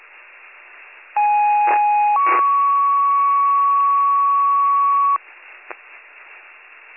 Two-tone page, heard as a one-second tone, then a three-second tone, about 830 then 1120 Hz. It is heard from 1.2 to 5.2 s.